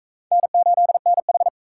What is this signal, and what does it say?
Morse code, heard as a keyed tone, about 35 wpm, 700 Hz. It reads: N8NH